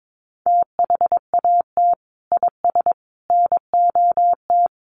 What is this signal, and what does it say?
Morse code, heard as a keyed tone, about 22 wpm, 700 Hz. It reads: THAT IS NOT
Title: Sentences from Top 600 Words 22wpm